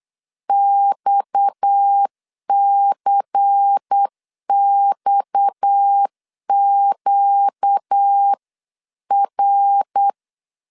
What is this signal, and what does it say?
Morse code, heard as a keyed tone, about 8 wpm, 800 Hz. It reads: XCXQR
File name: CW / DET / 1 6 1500C S B 8wpm 5142